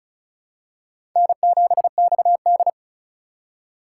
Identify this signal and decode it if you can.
Morse code, heard as a keyed tone, about 36 wpm, 700 Hz. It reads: N7XD
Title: USA Call Signs - 4 Characters - 3x Character Spacing - 35wpm